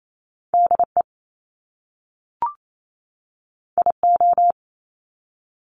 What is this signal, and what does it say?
Morse code, heard as a keyed tone, about 28 wpm, 700 Hz. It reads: DE IO
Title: Top Two Letter Combinations in Words 28wpm